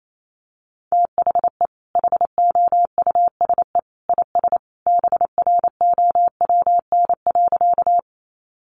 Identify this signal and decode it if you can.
Morse code, heard as a keyed tone, about 28 wpm, 700 Hz. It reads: THE HOUSE IS BROWN.
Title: Sentences from Top 1200 Words 28wpm